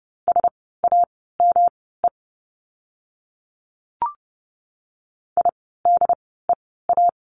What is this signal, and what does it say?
Morse code, heard as a keyed tone, about 30 wpm, spaced as for 17 wpm, 700 Hz. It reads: SAME IDEA
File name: course / 30:10wpm / 108 Lesson 18f - Letter m - Words 30wpm